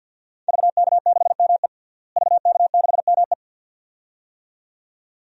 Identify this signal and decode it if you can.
Morse code, heard as a keyed tone, about 52 wpm, 700 Hz. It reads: VK6ME VK6ME
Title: DXCC-100 Call Signs - 5 Characters - Repeated 2x - 50wpm